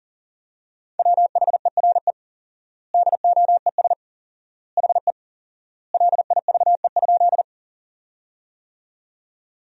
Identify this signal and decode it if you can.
Morse code, heard as a keyed tone, about 37 wpm, 700 Hz. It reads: WHERE DOES HE LIVE?
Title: 4x Word-Spacing - Sentences from Top 200 Words - 40wpm